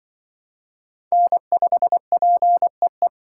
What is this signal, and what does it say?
Morse code, heard as a keyed tone, about 24 wpm, 700 Hz. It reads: N5PEE
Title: Speed Racing - USA 1x3 Call Signs 20wpm